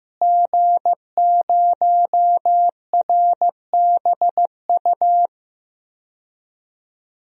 Morse code, heard as a keyed tone, about 15 wpm, 700 Hz. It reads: G0RBU
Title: DXCC-100 Call Signs - 5 Characters - Repeated 3x - 15wpm